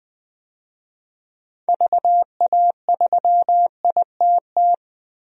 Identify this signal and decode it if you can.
Morse code, heard as a keyed tone, about 20 wpm, 700 Hz. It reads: VA3ITT
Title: DXCC-100 Call Signs - 6 Characters - 20wpm